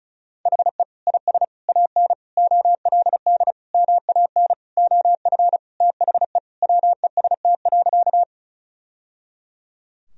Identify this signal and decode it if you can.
Morse code, heard as a keyed tone, about 36 wpm, 700 Hz. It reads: HE IS AN OLD MAN OF THE WEST.
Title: Sentences from Top 1000 Words - Review Part 2 - 35wpm